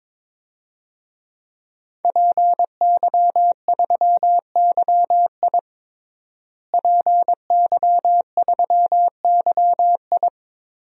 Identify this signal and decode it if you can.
Morse code, heard as a keyed tone, about 22 wpm, 700 Hz. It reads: PY3YI PY3YI
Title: DXCC-100 Call Signs - 5 Characters - Repeated 3x - 22wpm